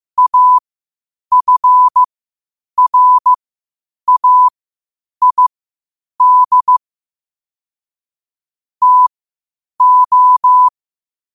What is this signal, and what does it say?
Morse code, heard as a keyed tone, about 15 wpm, spaced as for 8 wpm, 1000 Hz. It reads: AFRAID TO
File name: qotdfiles / 8wpm